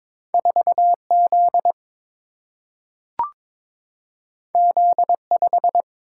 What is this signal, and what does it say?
Morse code, heard as a keyed tone, about 22 wpm, 700 Hz. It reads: 4Z Z5
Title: Single-Letter Single-Number 22wpm